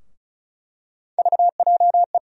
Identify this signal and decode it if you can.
Morse code, heard as a keyed tone, about 34 wpm, 700 Hz. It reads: VJE
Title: USA Call Sign Postfixes - 3 Letters 35wpm